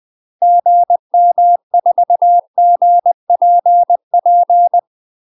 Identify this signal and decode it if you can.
Morse code, heard as a keyed tone, about 20 wpm, 700 Hz. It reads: GM4GPP
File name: UK MASTER QSOs TRAINING / 20WPM / UK MASTER QSOs 0011